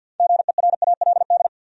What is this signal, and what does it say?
Morse code, heard as a keyed tone, about 51 wpm, 700 Hz. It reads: GERALD
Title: USA Common Names - 50wpm